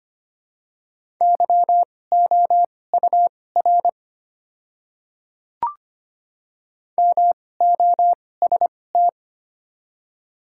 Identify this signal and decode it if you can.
Morse code, heard as a keyed tone, about 25 wpm, spaced as for 18 wpm, 700 Hz. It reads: YOUR MOST
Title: 2x Character Spacing - Top 100 Words - 25wpm